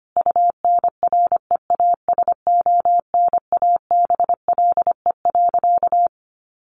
Morse code, heard as a keyed tone, about 25 wpm, 700 Hz. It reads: UNREASONABLE.
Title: A Sound Of Thunder 25wpm